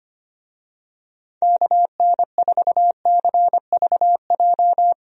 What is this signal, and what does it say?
Morse code, heard as a keyed tone, about 25 wpm, 700 Hz. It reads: KN4CVJ
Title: USA Call Signs - 6 Characters - Repeated 2x - 25wpm